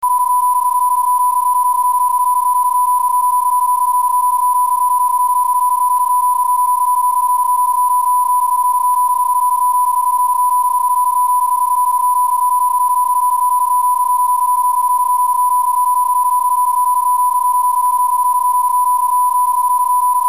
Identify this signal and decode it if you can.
PSK31, about 31.25 baud.